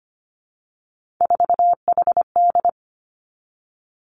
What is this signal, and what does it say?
Morse code, heard as a keyed tone, about 25 wpm, 700 Hz. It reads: 4HD
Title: Three Challenging Letter-Number Combinations 25wpm